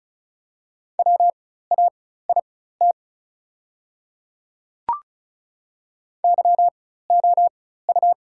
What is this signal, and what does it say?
Morse code, heard as a keyed tone, about 36 wpm, spaced as for 16 wpm, 690 Hz. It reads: WAIT YOU
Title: 4x Character Spacing - Top 500 Words - 35wpm